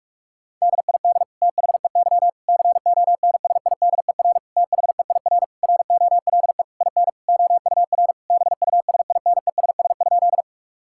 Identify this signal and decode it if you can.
Morse code, heard as a keyed tone, about 41 wpm, 700 Hz. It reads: DID THEY CONSIDER THEIR ROLE IN OUR BUSINESS?
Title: Sentences from Top 1000 Words - Review Part 6 - 45wpm